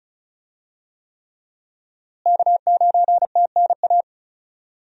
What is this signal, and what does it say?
Morse code, heard as a keyed tone, about 36 wpm, 700 Hz. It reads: K9TNA
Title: USA Call Signs - 5 Characters - Repeated 3x - 35wpm